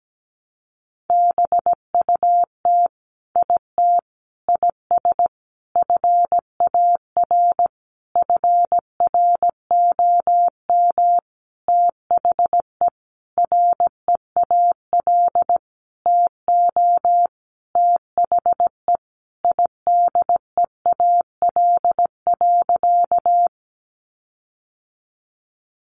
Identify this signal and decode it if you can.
Morse code, heard as a keyed tone, about 17 wpm, 700 Hz. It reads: BUT IT IS FAR FROM THE REAL TO THE IDEAL.
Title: Sentences from Top 1400 Words 17wpm